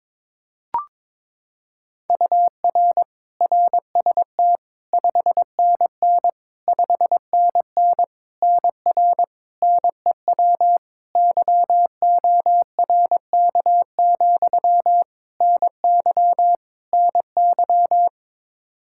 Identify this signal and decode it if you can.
Morse code, heard as a keyed tone, about 22 wpm, 700 Hz. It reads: UR RST 5NN 5NN NR NEW YORK, NY NY
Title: K1USN SST QSO Practice 22wpm